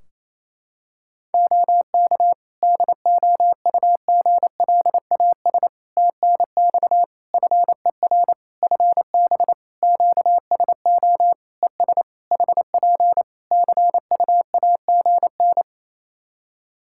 Morse code, heard as a keyed tone, about 28 wpm, 700 Hz. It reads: OK DOUGLAS TNX FER FB QSO ES HP CUAGN
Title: Practice QSO - Exchange 4 - Ending - 28wpm